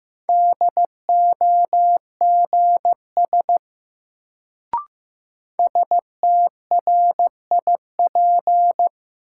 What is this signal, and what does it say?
Morse code, heard as a keyed tone, about 15 wpm, 700 Hz. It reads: DOGS STRIP
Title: Sentences from Top 1600 Words - New Word Drill - 15wpm